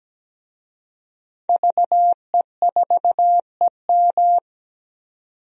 Morse code, heard as a keyed tone, about 17 wpm, 700 Hz. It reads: VE4EM
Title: DXCC-100 Call Signs - 5 Characters - 17wpm